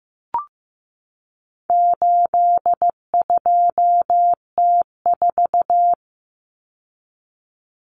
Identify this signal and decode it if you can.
Morse code, heard as a keyed tone, about 15 wpm, 700 Hz. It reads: 82T4
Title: Common Cut Numbers - 4-Digit 15wpm